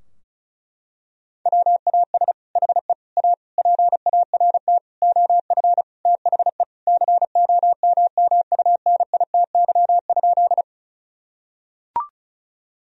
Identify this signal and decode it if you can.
Morse code, heard as a keyed tone, about 36 wpm, 700 Hz. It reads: WAS HE A PART OF THE COMMUNITY?
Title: Sentences from Top 1000 Words - Review Part 4 - 35wpm